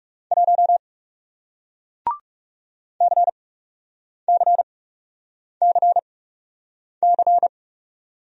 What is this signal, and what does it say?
Morse code, heard as a keyed tone, about 41 wpm, spaced as for 8 wpm, 700 Hz. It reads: 1 CCCC